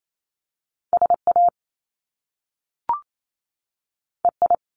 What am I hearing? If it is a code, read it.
Morse code, heard as a keyed tone, about 28 wpm, 700 Hz. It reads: SA EI
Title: Top Two Letter Combinations in Words 28wpm